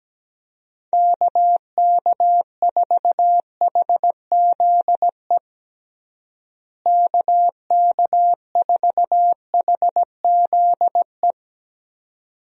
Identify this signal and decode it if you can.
Morse code, heard as a keyed tone, about 17 wpm, 700 Hz. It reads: KK4HZE KK4HZE
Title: USA Call Signs - 6 Characters - Repeated 2x - 17wpm